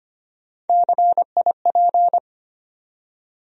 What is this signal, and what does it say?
Morse code, heard as a keyed tone, about 25 wpm, 700 Hz. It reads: CIP
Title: USA Call Sign Postfixes - 3 Letters 25wpm